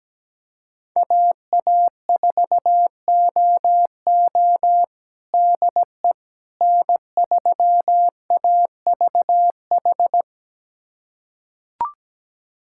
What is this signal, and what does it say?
Morse code, heard as a keyed tone, about 17 wpm, 700 Hz. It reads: AA4OO DE N3AVH
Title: USA Callsign DE Callsign - 5 Characters - 17wpm